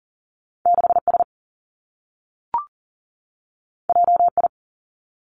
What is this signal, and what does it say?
Morse code, heard as a keyed tone, about 40 wpm, 700 Hz. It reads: 6 S J I
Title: Two Challenging Letter-Number Combinations 40wpm